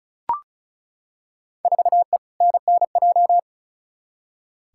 Morse code, heard as a keyed tone, about 34 wpm, 700 Hz. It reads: 4E NNJ
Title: Field Day Exchanges - 35wpm